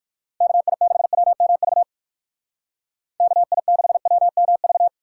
Morse code, heard as a keyed tone, about 64 wpm, spaced as for 56 wpm, 700 Hz. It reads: KI6WMV KI6WMV
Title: Speed Racing - USA 2x3 Call Signs 45wpm